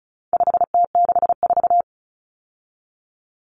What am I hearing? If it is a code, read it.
Morse code, heard as a keyed tone, about 34 wpm, 700 Hz. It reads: 5T64
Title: Common Cut Numbers - 4-Digit 35wpm